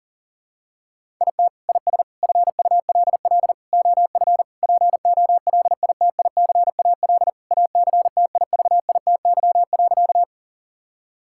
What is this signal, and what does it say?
Morse code, heard as a keyed tone, about 37 wpm, 700 Hz. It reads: IT IS FULL OF POLITICAL ACTIVITY.
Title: Sentences from Top 1000 Words - Review Part 5 - 40wpm